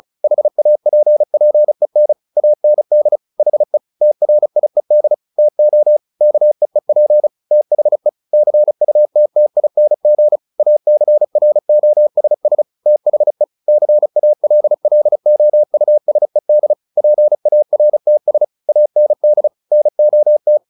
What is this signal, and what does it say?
Morse code, heard as a keyed tone, about 34 wpm, 600 Hz. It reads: HAPPEN AND HE TRIED TO KEEP THE CUTTING ACROSS THE CALLOUSED PARTS AND NOT
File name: the old man and the sea 2 35wpm